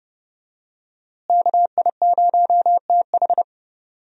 Morse code, heard as a keyed tone, about 32 wpm, 700 Hz. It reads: KI0TH
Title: USA Call Signs - 5 Characters - Repeated 3x - 30wpm